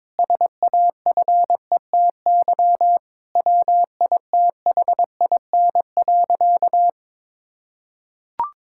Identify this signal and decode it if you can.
Morse code, heard as a keyed tone, about 22 wpm, 700 Hz. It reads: SAFETY WITHIN.
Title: Sentences from Top 1600 Words 22wpm